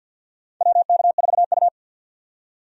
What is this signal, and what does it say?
Morse code, heard as a keyed tone, about 51 wpm, 700 Hz. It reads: WK4U
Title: USA Call Signs - 4 Characters - 50wpm